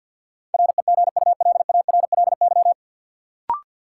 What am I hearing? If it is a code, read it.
Morse code, heard as a keyed tone, about 53 wpm, 700 Hz. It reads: REGULARLY
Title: Sentences from Top 1400 Words - New Word Drill Alt - 50wpm